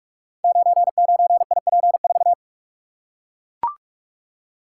Morse code, heard as a keyed tone, about 43 wpm, 700 Hz. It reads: 99IP4
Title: Code Groups - 5 Letter:Numbers - 45wpm